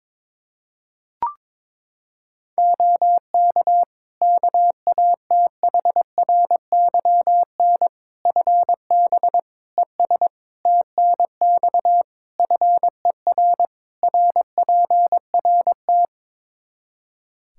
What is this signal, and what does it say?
Morse code, heard as a keyed tone, about 22 wpm, 700 Hz. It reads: OK KATHRYN FB ES TNX FER RPRT
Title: K1USN SST QSO Practice 22wpm